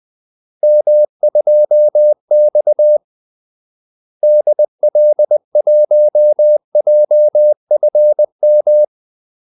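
Morse code, heard as a keyed tone, about 20 wpm, 600 Hz. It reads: M2X DL1JFM
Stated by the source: Morse call-sign practice at 20 wpm.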